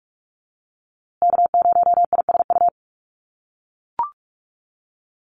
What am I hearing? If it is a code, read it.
Morse code, heard as a keyed tone, about 44 wpm, 700 Hz. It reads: K0ISU
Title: USA Call Signs Repeated 2x 45wpm